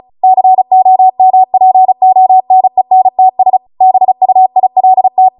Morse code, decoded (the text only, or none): COMPONENTS BUILT